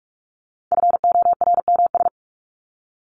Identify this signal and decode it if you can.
Morse code, heard as a keyed tone, about 44 wpm, 700 Hz. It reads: FORMS
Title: ES:S Suffix - Top 100 Words with Base Words 45wpm